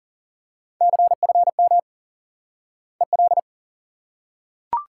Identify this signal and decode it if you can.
Morse code, heard as a keyed tone, about 40 wpm, 700 Hz. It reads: CFM EL
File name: IQE Course 3 - Lesson 89 - YAGI - Two Elements - 40wpm